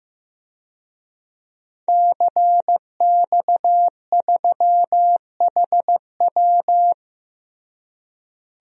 Morse code, heard as a keyed tone, about 15 wpm, 700 Hz. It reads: CX3HW
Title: DXCC-100 Call Signs - 5 Characters - Repeated 3x - 15wpm